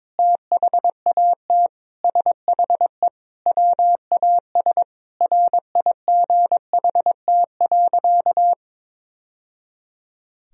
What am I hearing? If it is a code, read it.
Morse code, heard as a keyed tone, about 22 wpm, 700 Hz. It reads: THAT SHE WAS RIGHT.
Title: Sentences from Top 1000 Words - Review Part 1 - 22wpm